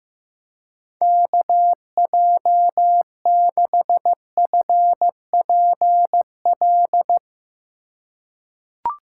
Morse code, heard as a keyed tone, about 15 wpm, 700 Hz. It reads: KJ6FPL
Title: USA Call Signs - 6 Characters - 4x Character Spacing - 15wpm